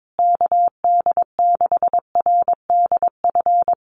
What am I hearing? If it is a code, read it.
Morse code, heard as a keyed tone, about 22 wpm, 700 Hz. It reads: KD6RDF
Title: USA Call Signs Repeated 3x 22wpm